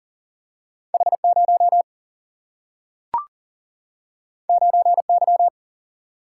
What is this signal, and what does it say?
Morse code, heard as a keyed tone, about 40 wpm, 700 Hz. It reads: H0 9Y
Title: Single-Letter Single-Number 40wpm